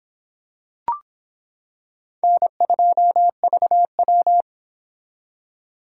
Morse code, heard as a keyed tone, about 23 wpm, 700 Hz. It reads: N2VW